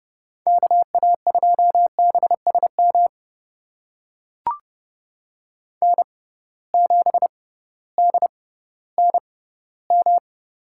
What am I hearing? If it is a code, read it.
Morse code, heard as a keyed tone, about 30 wpm, spaced as for 16 wpm, 700 Hz. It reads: KA2BSM N7DNM
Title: USA Call Signs - 6x Character Spacing - 30wpm